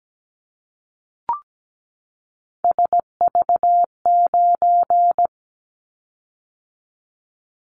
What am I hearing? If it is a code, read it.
Morse code, heard as a keyed tone, about 17 wpm, 700 Hz. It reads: SV9
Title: Callsign Prefixes for Top 100 DXCC - 17wpm